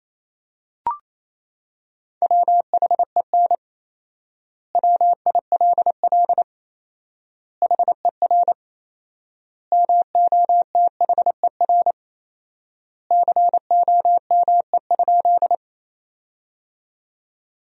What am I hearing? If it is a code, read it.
Morse code, heard as a keyed tone, about 27 wpm, 700 Hz. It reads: WHEN WILL HER MOTHER COME?
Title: 4x Word-Spacing - Sentences from Top 200 Words - 28wpm